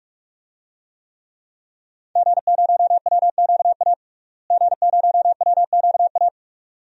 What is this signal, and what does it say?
Morse code, heard as a keyed tone, about 46 wpm, 700 Hz. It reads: G0WQA G0WQA
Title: DXCC-100 Call Signs - 5 Characters - Repeated 2x - 45wpm